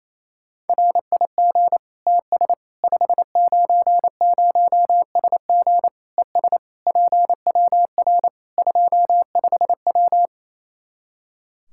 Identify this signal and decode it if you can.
Morse code, heard as a keyed tone, about 28 wpm, 700 Hz. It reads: RIG TS 590SG ES PWR 25W
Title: Practice QSO - Exchange 2 - Info - 28wpm